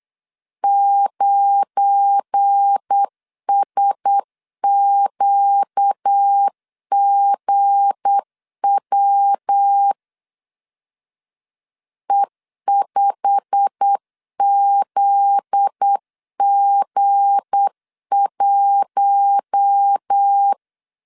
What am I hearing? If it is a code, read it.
Morse code, heard as a keyed tone, about 8 wpm, 800 Hz. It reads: 9SQGW E5ZG1
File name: CW / DET / 1 9 1500C S C 8wpm 5653